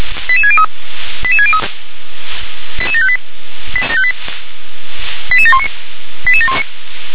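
Five-tone selective call (ZVEI), heard as quick runs of five tones, tones of about 70 ms. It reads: 88603 88603 88758 88719 88719
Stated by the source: ZVEI five-tone selective call 88714 received via SDR.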